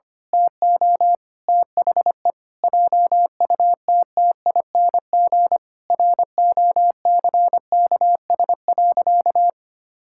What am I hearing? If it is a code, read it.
Morse code, heard as a keyed tone, about 25 wpm, 700 Hz. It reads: TO THE JUTTING ROCKS.
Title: Pangram Challenge v2 - 25wpm